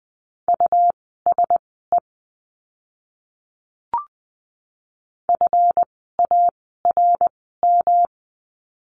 Morse code, heard as a keyed tone, about 20 wpm, spaced as for 14 wpm, 700 Hz. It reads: USE FARM
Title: 139 Lesson 23f - Letter f - Words 20wpm